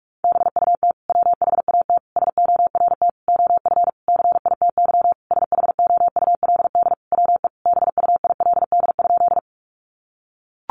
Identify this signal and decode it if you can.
Morse code, heard as a keyed tone, about 44 wpm, 700 Hz. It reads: BUT WHAT SORT OF CITY SHOULD WE BUILD?
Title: Sentences from Top 1300 Words 45wpm